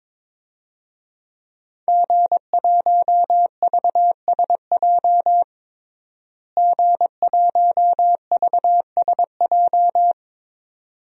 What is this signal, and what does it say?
Morse code, heard as a keyed tone, about 22 wpm, 700 Hz. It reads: G1VSJ G1VSJ
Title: DXCC-100 Call Signs - 5 Characters - Repeated 2x - 22wpm